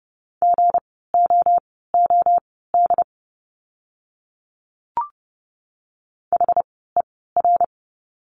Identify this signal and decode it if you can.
Morse code, heard as a keyed tone, about 30 wpm, spaced as for 17 wpm, 700 Hz. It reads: GOOD HER
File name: course / 30:10wpm / 230 Lesson 37f - Letter x - Words 30wpm